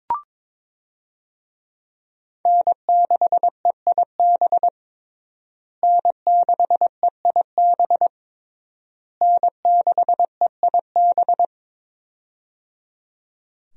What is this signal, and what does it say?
Morse code, heard as a keyed tone, about 22 wpm, 700 Hz. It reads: N6EIB N6EIB N6EIB
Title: USA Call Signs - 5 Characters - Repeated 3x - 22wpm